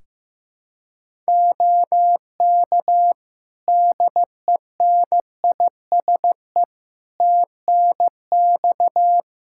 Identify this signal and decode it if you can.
Morse code, heard as a keyed tone, about 15 wpm, 700 Hz. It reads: OK DENISE TNX
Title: K1USN SST QSO Practice 15wpm